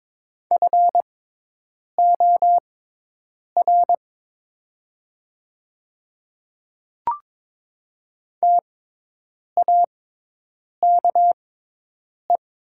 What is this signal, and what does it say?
Morse code, heard as a keyed tone, about 22 wpm, spaced as for 8 wpm, 700 Hz. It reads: FOR TAKE
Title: 6x Character Spacing - Top 300 Words - 22wpm